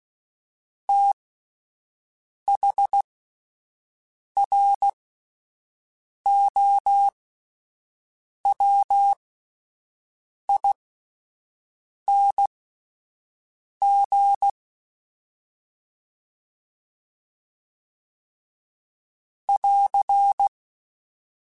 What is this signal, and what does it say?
Morse code, heard as a keyed tone, about 16 wpm, spaced as for 5 wpm, 800 Hz. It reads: THROWING <AR>